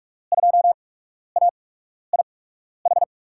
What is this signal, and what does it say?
Morse code, heard as a keyed tone, about 46 wpm, spaced as for 12 wpm, 700 Hz. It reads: 2AIH
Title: USA Call Signs - 6 Characters - 8x Character Spacing - 45wpm